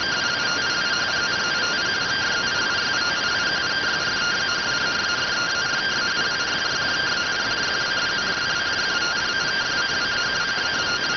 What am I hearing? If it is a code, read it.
Two-tone teleprinter signal, about 170 Hz shift.